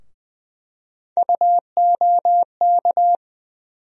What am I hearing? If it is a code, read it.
Morse code, heard as a keyed tone, about 20 wpm, 700 Hz. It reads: UOK
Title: USA Call Sign Postfixes - 3 Letters 20wpm